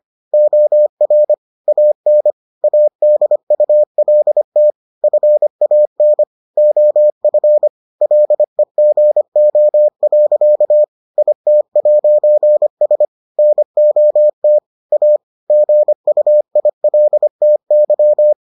Morse code, heard as a keyed tone, about 25 wpm, 600 Hz. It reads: OR AN ADULT FAN OF LEGO. IT'S NOT A GUILTY